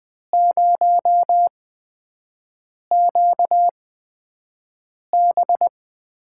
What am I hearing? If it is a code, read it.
Morse code, heard as a keyed tone, about 20 wpm, spaced as for 5 wpm, 700 Hz. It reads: 0QB
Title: USA Call Signs - 5 Characters - 8x Character Spacing - 20wpm